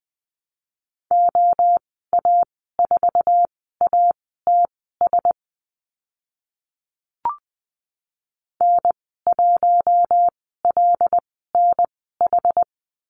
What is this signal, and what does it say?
Morse code, heard as a keyed tone, about 20 wpm, spaced as for 14 wpm, 700 Hz. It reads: OA4ATS N1LNH
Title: 082 Lesson 14g - Number 2 - Call Signs 20wpm